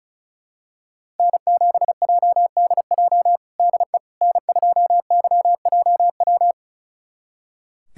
Morse code, heard as a keyed tone, about 34 wpm, 700 Hz. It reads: N7JDJ DE N2YJW
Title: USA Callsign DE Callsign 35wpm